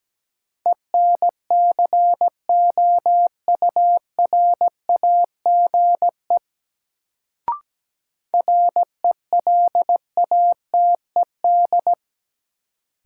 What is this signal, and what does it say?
Morse code, heard as a keyed tone, about 17 wpm, 700 Hz. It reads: ENCOURAGE RELATED
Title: Sentences from Top 1400 Words - New Word Drill Alt - 17wpm